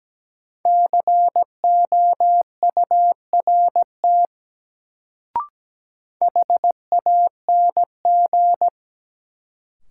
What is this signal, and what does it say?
Morse code, heard as a keyed tone, about 17 wpm, 700 Hz. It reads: COURT HANG
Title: Sentences from Top 1600 Words - New Word Drill - 17wpm